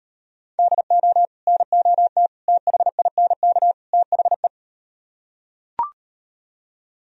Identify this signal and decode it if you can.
Morse code, heard as a keyed tone, about 39 wpm, 700 Hz. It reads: DO NOT THINK THE